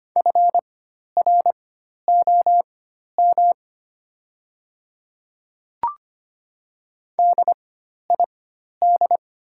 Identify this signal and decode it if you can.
Morse code, heard as a keyed tone, about 25 wpm, spaced as for 12 wpm, 700 Hz. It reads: FROM DID
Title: 4x Character Spacing - Top 100 Words - 25wpm